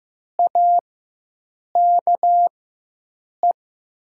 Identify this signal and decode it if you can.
Morse code, heard as a keyed tone, about 15 wpm, spaced as for 7 wpm, 700 Hz. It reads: AKE
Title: 4x Character Spacing - Top 100 Words - 15wpm